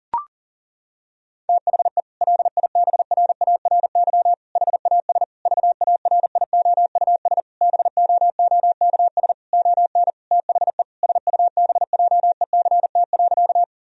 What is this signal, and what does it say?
Morse code, heard as a keyed tone, about 40 wpm, 700 Hz. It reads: THE LIBRARY HAS VARIOUS BOOKS ON THE SUBJECT.
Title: Sentences from Top 1000 Words - Review Part 4 - 40wpm